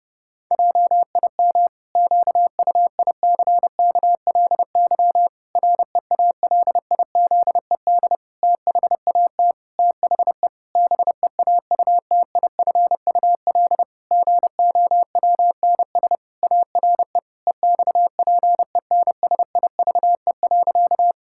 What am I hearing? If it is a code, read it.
Morse code, heard as a keyed tone, about 30 wpm, 700 Hz. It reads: JIM QUICKLY REALIZED THAT THE BEAUTIFUL GOWNS ARE EXPENSIVE.
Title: Pangram Challenge v2 - 30wpm